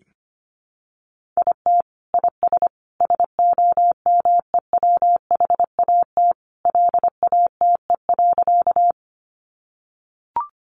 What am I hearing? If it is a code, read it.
Morse code, heard as a keyed tone, about 25 wpm, 700 Hz. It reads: IT IS SOMEWHAT LATE.
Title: Sentences from Top 1300 Words 25wpm